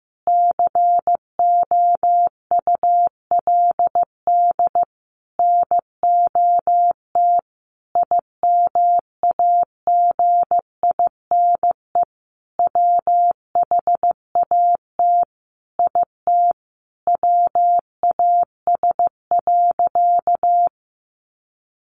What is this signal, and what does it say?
Morse code, heard as a keyed tone, about 15 wpm, 700 Hz. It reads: COULD NOT IMAGINE WHAT IT WAS.
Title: Sentences from Top 1400 Words 15wpm